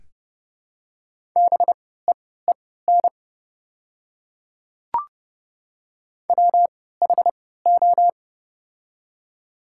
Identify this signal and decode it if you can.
Morse code, heard as a keyed tone, about 30 wpm, spaced as for 17 wpm, 700 Hz. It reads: BEEN WHO
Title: 3x Character Spacing - Top 100 Words - 30wpm